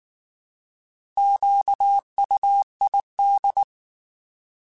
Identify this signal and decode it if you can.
Morse code, heard as a keyed tone, about 19 wpm, 800 Hz. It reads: QUID
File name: Alphabetical / Alphabetical mp3s / Qs @18